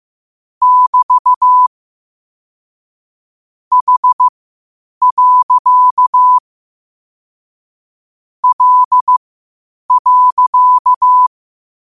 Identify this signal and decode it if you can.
Morse code, heard as a keyed tone, about 15 wpm, spaced as for 8 wpm, 1000 Hz. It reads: <BT> H. L.
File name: qotdfiles / 8wpm